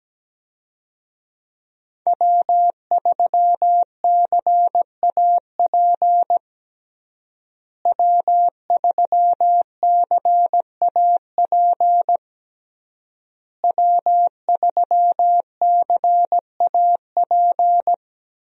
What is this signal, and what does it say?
Morse code, heard as a keyed tone, about 17 wpm, 700 Hz. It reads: W3CAP W3CAP W3CAP
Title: USA Call Signs - 5 Characters - Repeated 3x - 17wpm